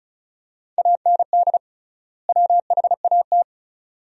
Morse code, heard as a keyed tone, about 36 wpm, 700 Hz. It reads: AND WHAT
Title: 3x Word-Spacing - Sentences from Top 200 Words - 35wpm